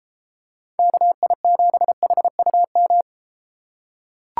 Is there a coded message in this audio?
Yes: Morse code from 0.8 to 3.0 s, heard as a keyed tone, about 31 wpm, 700 Hz. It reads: KI7HUM